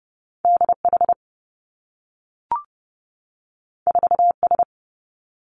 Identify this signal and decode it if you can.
Morse code, heard as a keyed tone, about 30 wpm, 700 Hz. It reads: D H 4 S
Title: Two Challenging Letter-Number Combinations 30wpm